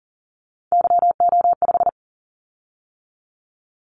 Morse code, heard as a keyed tone, about 37 wpm, 700 Hz. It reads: YO5
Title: Callsign Prefixes for Top 25 DXCC - 40wpm